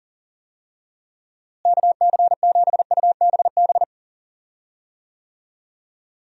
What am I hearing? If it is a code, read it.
Morse code, heard as a keyed tone, about 37 wpm, 700 Hz. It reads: KC7UBB
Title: USA Call Signs - 6 Characters - Repeated 2x - 40wpm